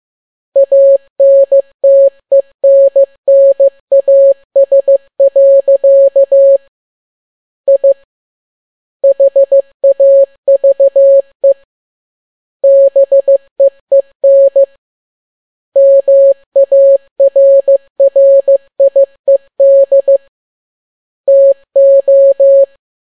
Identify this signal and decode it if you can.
Morse code, heard as a keyed tone, about 15 wpm, 550 Hz. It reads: ANTENNAS. I HAVE BEEN MARRIED TO